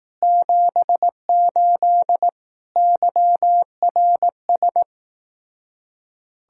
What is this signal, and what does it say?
Morse code, heard as a keyed tone, about 18 wpm, 700 Hz. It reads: 78 YRS
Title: Speed-Racing Common QSO Phrases Encore - 15wpm